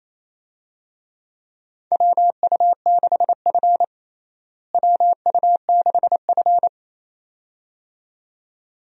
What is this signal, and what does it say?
Morse code, heard as a keyed tone, about 28 wpm, 700 Hz. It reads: WU6F WU6F
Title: USA Call Signs - 4 Characters - Repeated 2x - 28wpm